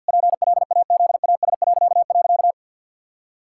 Morse code, heard as a keyed tone, about 50 wpm, 700 Hz. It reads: PLAZAS..
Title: Pangram Challenge v2 - 50wpm